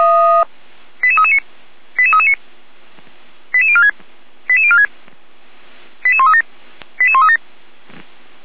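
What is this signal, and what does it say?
Five-tone selective call (ZVEI), heard as quick runs of five tones, tones of about 70 ms. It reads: 88339 88339 88046 88046 88137 88137
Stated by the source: ZVEI five-tone selective call 88341 received on an SDR, Feuerwehr Orlamünde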